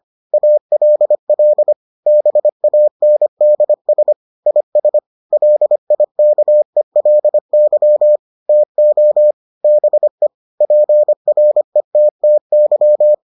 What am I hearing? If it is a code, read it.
Morse code, heard as a keyed tone, about 25 wpm, 600 Hz. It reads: ALL BANDS IS LIKELY TO BE PRETTY